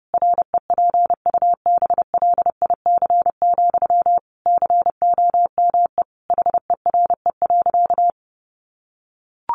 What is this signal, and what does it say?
Morse code, heard as a keyed tone, about 30 wpm, 700 Hz. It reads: REPUBLIC, COME HERE.
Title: Sentences from Top 1500 Words 30wpm